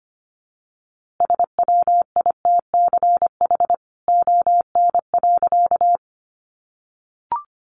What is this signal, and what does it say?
Morse code, heard as a keyed tone, about 25 wpm, 700 Hz. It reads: SWITCH ON.
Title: A Sound Of Thunder 25wpm